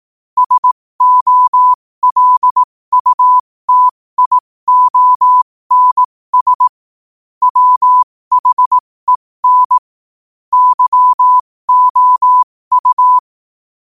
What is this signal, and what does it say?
Morse code, heard as a keyed tone, about 18 wpm, spaced as for 15 wpm, 1000 Hz. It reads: SOLUTIONS WHEN YOU
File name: qotdfiles / 15wpm